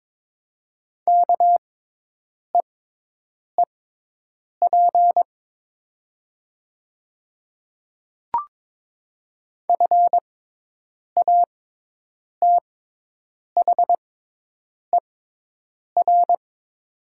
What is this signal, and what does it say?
Morse code, heard as a keyed tone, about 22 wpm, spaced as for 8 wpm, 700 Hz. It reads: KEEP FATHER
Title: 6x Character Spacing - Top 300 Words - 22wpm